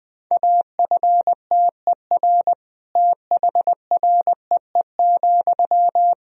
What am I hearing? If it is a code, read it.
Morse code, heard as a keyed tone, about 21 wpm, 700 Hz. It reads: AFTER THREE,
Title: Sentences from Top 200 Words and Abbreviations 20wpm